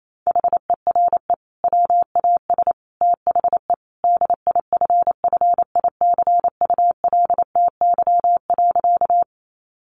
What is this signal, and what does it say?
Morse code, heard as a keyed tone, about 28 wpm, 700 Hz. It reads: HERE WAS THE DIFFICULTY.
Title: Sentences from Top 1500 Words 28wpm